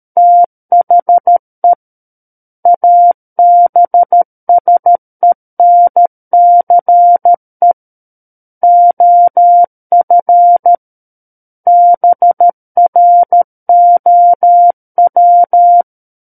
Morse code, heard as a keyed tone, about 13 wpm, 700 Hz. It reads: THE ABSENCE OF BROW
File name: Book One / 04-Chapter 4 - 13 WPM